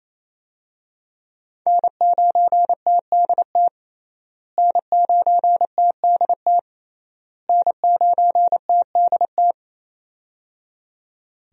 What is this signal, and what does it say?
Morse code, heard as a keyed tone, about 28 wpm, 700 Hz. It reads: N9TDT N9TDT N9TDT
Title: USA Call Signs - 5 Characters - Repeated 3x - 28wpm